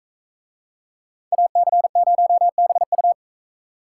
Morse code, heard as a keyed tone, about 44 wpm, 700 Hz. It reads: AC0BU